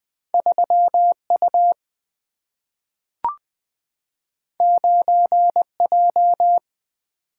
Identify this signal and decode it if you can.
Morse code, heard as a keyed tone, about 20 wpm, 700 Hz. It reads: 3 U 9 J
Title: Single-Letter Single-Number 20wpm